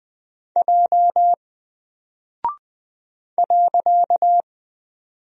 Morse code, heard as a keyed tone, about 20 wpm, 700 Hz. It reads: J .
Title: Single Character (Letter:Number:Punctuation:Prosign) 20wpm